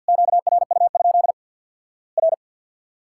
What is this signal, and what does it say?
Morse code, heard as a keyed tone, about 52 wpm, 700 Hz. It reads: QRU? R
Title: Worked All Europe DX Contest - QTC Exchange with Repeats - 50wpm